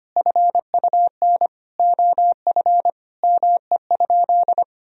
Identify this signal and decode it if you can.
Morse code, heard as a keyed tone, about 25 wpm, 700 Hz. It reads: FUN OF ME?
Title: Sentences from Top 1000 Words - Review Part 4 - 25wpm